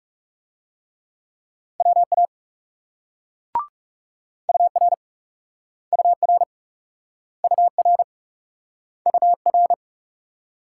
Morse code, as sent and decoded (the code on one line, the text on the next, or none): .-- .- / ..- .-. / ..- .-. / ..- .-. / ..- .-.
WA UR UR UR UR